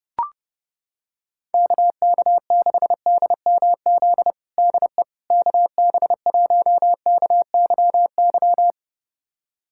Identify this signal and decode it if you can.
Morse code, heard as a keyed tone, about 30 wpm, 700 Hz. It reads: KK6DMZ DE KB1KYY
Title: USA Callsign DE Callsign - 6 Characters - 30wpm